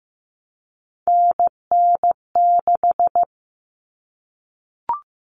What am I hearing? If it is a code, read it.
Morse code, heard as a keyed tone, about 15 wpm, 700 Hz. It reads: NN6
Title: Common Cut Numbers - 3-Digit 15wpm